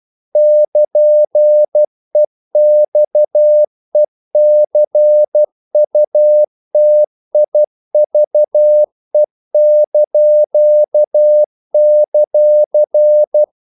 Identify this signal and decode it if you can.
Morse code, heard as a keyed tone, about 12 wpm, 600 Hz. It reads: <KN>EXECUTIVE);
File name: CW / Practice / NationalCapitals12